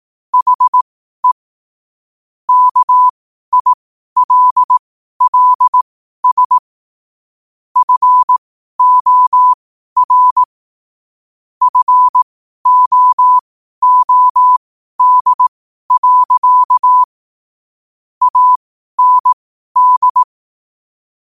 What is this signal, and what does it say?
Morse code, heard as a keyed tone, about 18 wpm, spaced as for 12 wpm, 1000 Hz. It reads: HE KILLS FOR FOOD. AND
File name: qotdfiles / 12wpm